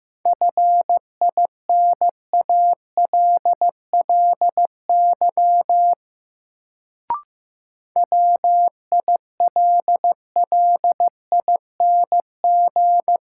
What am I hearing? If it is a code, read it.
Morse code, heard as a keyed tone, about 15 wpm, 700 Hz. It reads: FINALLY WILLING
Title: Sentences from Top 1000 Words - New Word Drill - 15wpm